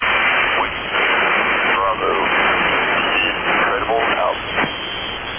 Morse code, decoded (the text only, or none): none